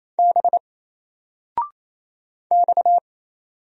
Morse code, heard as a keyed tone, about 27 wpm, 700 Hz. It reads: B X
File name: Single-Letter-Number-28wpm